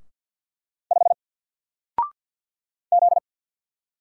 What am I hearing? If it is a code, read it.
Morse code, heard as a keyed tone, about 50 wpm, 700 Hz. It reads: 5 Z
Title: Single Character (Letter:Number:Punctuation:Prosign) 50wpm